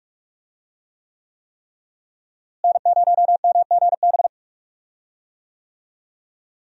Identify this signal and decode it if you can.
Morse code, heard as a keyed tone, about 46 wpm, 700 Hz. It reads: N0MGB
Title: USA Call Signs - 5 Characters - Repeated 3x - 45wpm